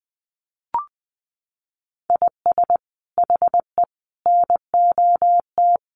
Morse code, heard as a keyed tone, about 20 wpm, 700 Hz. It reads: IS HE NOT
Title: Sentences from Top 1500 Words 20wpm